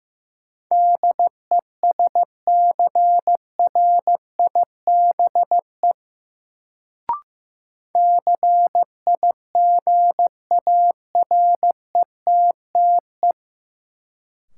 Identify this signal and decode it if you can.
Morse code, heard as a keyed tone, about 15 wpm, 700 Hz. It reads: DESCRIBE CIGARETTE
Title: Sentences from Top 1000 Words - New Word Drill - 15wpm